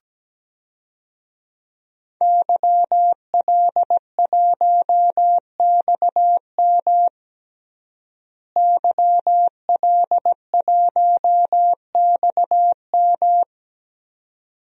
Morse code, heard as a keyed tone, about 17 wpm, 700 Hz. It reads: YL1XM YL1XM
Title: DXCC-100 Call Signs - 5 Characters - Repeated 2x - 17wpm